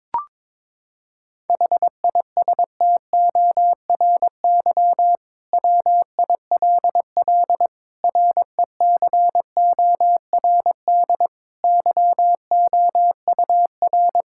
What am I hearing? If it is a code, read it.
Morse code, heard as a keyed tone, about 22 wpm, 700 Hz. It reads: HISTORY WILL RECORD YOUR
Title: Sentences from Top 1600 Words 22wpm